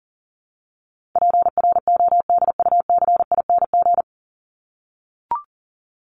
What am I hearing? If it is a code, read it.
Morse code, heard as a keyed tone, about 40 wpm, 700 Hz. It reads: PRODUCING
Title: ING Suffix - Top 100 Words with Base Words 40wpm